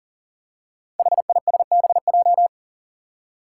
Morse code, heard as a keyed tone, about 40 wpm, 700 Hz. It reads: HISBJ
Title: Five Challenging Letter-Number Combinations 40wpm